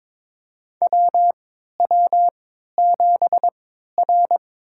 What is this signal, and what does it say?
Morse code, heard as a keyed tone, about 22 wpm, spaced as for 12 wpm, 700 Hz. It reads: WW7R
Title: USA Call Signs - 4 Characters - 3x Character Spacing - 22wpm